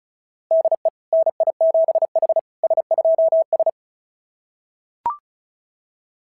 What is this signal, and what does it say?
Morse code, heard as a keyed tone, about 36 wpm, 660 Hz. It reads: DE NI7H S2S
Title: SOTA QSO Practice 35wpm